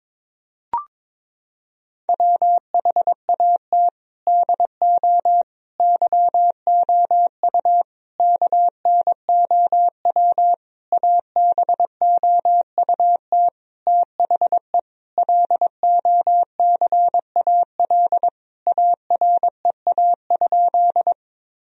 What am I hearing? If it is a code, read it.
Morse code, heard as a keyed tone, about 22 wpm, 700 Hz. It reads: WHAT DO YOU KNOW ABOUT THE LOCAL AREA?
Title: Sentences from Top 1000 Words - Review Part 3 - 22wpm